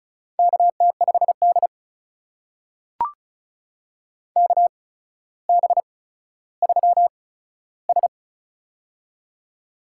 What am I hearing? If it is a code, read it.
Morse code, heard as a keyed tone, about 35 wpm, spaced as for 15 wpm, 700 Hz. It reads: KT5D KB3S
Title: USA Call Signs - 4 Characters - 8x Character Spacing - 35wpm